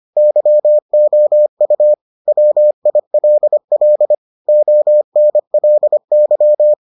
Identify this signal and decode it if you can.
Morse code, heard as a keyed tone, about 25 wpm, 600 Hz. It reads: YOU WILL ONLY